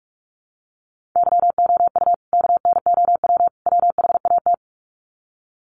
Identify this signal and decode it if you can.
Morse code, heard as a keyed tone, about 44 wpm, 700 Hz. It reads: YOU KNOW WHAT
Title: Sets of 3 Words 45wpm